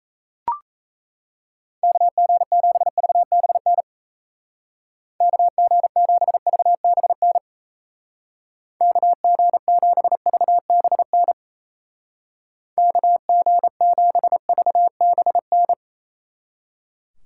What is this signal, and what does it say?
Morse code, heard as a keyed tone, about 34 wpm, 700 Hz. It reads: KG7VBN KG7VBN KG7VBN KG7VBN
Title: Speed Racing - USA 2x3 Call Signs 28wpm